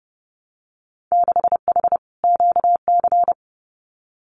Morse code, heard as a keyed tone, about 30 wpm, 700 Hz. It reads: 6H QC
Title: Winter Field Day Exchanges Encore 30wpm